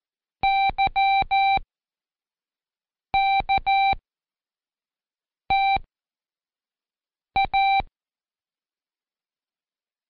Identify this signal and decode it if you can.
Morse code, heard as a keyed tone, about 14 wpm, spaced as for 5 wpm, 780 Hz. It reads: YKTA
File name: Code Quick / AudioDownld1 / Seg18-Day2-Quiz ASYO